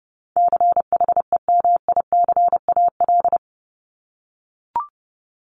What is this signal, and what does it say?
Morse code, heard as a keyed tone, about 30 wpm, 700 Hz. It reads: CHEMICAL
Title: Top 1001-1250 Words 30wpm